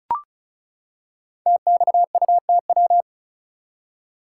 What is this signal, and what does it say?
Morse code, heard as a keyed tone, about 36 wpm, 700 Hz. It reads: TXUTW
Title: Code Groups - 5 Letters - 35wpm